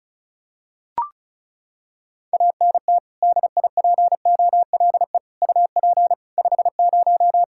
Morse code, heard as a keyed tone, about 36 wpm, 700 Hz. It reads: ANT DIPOLE UP 50
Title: Practice QSO - Exchange 2 - Info - 35wpm